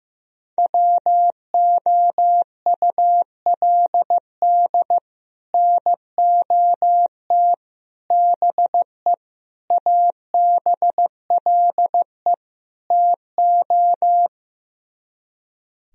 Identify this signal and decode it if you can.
Morse code, heard as a keyed tone, about 15 wpm, 700 Hz. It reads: WOULD NOT BE ABLE TO
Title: Speed-Racing - Sets of 5 Words 15wpm